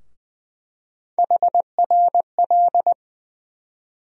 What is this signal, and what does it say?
Morse code, heard as a keyed tone, about 20 wpm, 700 Hz. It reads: HRL
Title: USA Call Sign Postfixes - 3 Letters 20wpm